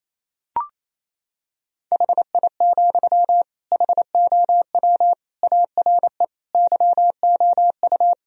Morse code, heard as a keyed tone, about 28 wpm, 700 Hz. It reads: HI, HOW ARE YOU
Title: Sentences from Top 1000 Words - Review Part 3 - 28wpm